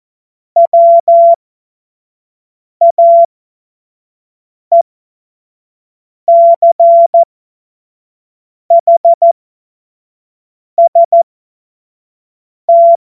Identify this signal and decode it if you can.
Morse code, heard as a keyed tone, about 14 wpm, spaced as for 5 wpm, 680 Hz. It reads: WAECHST
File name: Morsekurs / Witze / Witz 09. November 2021, 25 F70